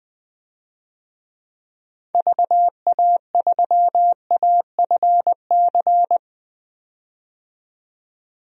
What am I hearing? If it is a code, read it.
Morse code, heard as a keyed tone, about 21 wpm, 700 Hz. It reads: VA3AFC
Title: DXCC-100 Call Signs - 6 Characters - 20wpm